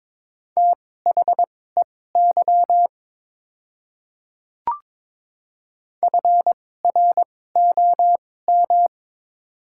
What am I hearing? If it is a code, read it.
Morse code, heard as a keyed tone, about 23 wpm, spaced as for 16 wpm, 690 Hz. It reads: THEY FROM
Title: 2x Character Spacing - Top 100 Words - 22wpm